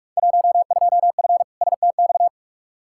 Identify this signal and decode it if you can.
Morse code, heard as a keyed tone, about 41 wpm, 700 Hz. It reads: 12F STX
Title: Field Day Exchanges Encore - 45wpm